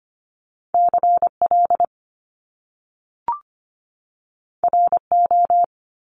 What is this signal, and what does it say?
Morse code, heard as a keyed tone, about 25 wpm, 700 Hz. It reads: CL RO
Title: Top Two Letter Combinations in Words 25wpm